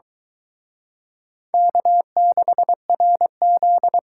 Morse code, heard as a keyed tone, about 24 wpm, 700 Hz. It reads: K6RZ